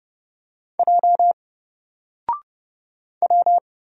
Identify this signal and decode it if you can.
Morse code, heard as a keyed tone, about 29 wpm, 700 Hz. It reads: J W
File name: Single-Letter-Number-30wpm